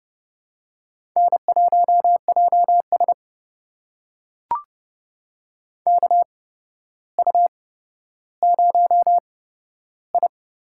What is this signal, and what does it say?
Morse code, heard as a keyed tone, about 29 wpm, spaced as for 13 wpm, 700 Hz. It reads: N1JS KU0I
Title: USA Call Signs - 4 Characters - 8x Character Spacing - 30wpm